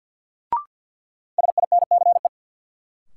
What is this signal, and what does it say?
Morse code, heard as a keyed tone, about 50 wpm, 700 Hz. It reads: SINCE
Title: Sentences from Top 300 Words - New Word Drill - 50wpm